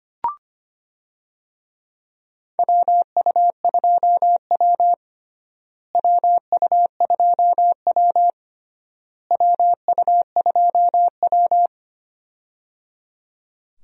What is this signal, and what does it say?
Morse code, heard as a keyed tone, about 25 wpm, 700 Hz. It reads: WU2W WU2W WU2W
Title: USA Call Signs - 4 Characters - Repeated 3x - 25wpm